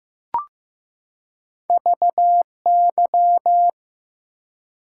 Morse code, heard as a keyed tone, about 15 wpm, 700 Hz. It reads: VY
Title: Top 90 QSO Elements 15wpm